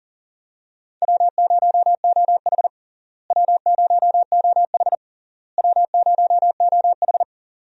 Morse code, heard as a keyed tone, about 40 wpm, 700 Hz. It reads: W0OH W0OH W0OH
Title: USA Call Signs - 4 Characters - Repeated 3x - 40wpm